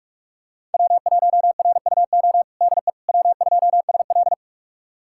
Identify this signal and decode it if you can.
Morse code, heard as a keyed tone, about 45 wpm, 700 Hz. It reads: W1RUO DE W2SL
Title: USA Callsign DE Callsign 45wpm